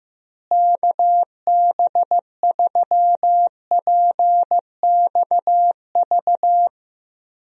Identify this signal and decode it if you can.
Morse code, heard as a keyed tone, about 15 wpm, 700 Hz. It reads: KB3PXV
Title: USA Call Signs - 6 Characters - Repeated 3x - 15wpm